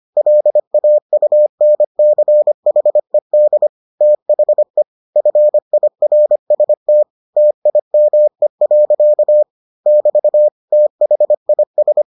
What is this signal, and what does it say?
Morse code, heard as a keyed tone, about 25 wpm, 600 Hz. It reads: LAUNCHED THE FIRST TIME. <BT> THIS